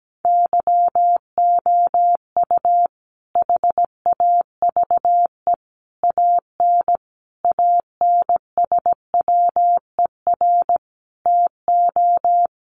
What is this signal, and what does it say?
Morse code, heard as a keyed tone, about 17 wpm, 700 Hz. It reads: YOU HAVE AN ANSWER TO
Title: Sentences from Top 700 Words 17wpm